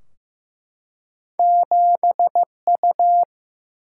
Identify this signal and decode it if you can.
Morse code, heard as a keyed tone, about 15 wpm, 700 Hz. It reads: 7U
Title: Single-Letter Single-Number 15wpm